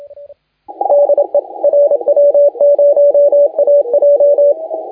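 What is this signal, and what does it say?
Morse code, heard as a keyed tone, about 28 wpm, 580 Hz, with radio noise. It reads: DE RW0AJ
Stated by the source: Morse contest audio recorded by LA4C during the SAC CW 2018 contest.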